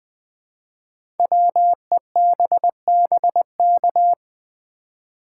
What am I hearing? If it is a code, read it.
Morse code, heard as a keyed tone, about 20 wpm, 700 Hz. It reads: WEBBK
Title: Code Groups - 5 Letter:Numbers - 20wpm